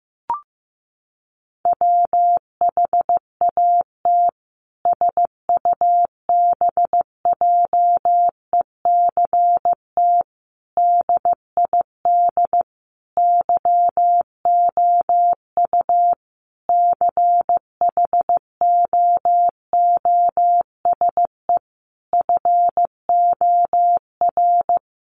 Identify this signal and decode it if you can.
Morse code, heard as a keyed tone, about 15 wpm, 700 Hz. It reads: WHAT SUBJECT DID YOU CHOOSE FOR
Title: Sentences from Top 1500 Words 15wpm